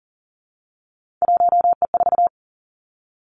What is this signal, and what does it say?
Morse code, heard as a keyed tone, about 40 wpm, 700 Hz. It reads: 1E4
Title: Three Challenging Letter-Number Combinations 40wpm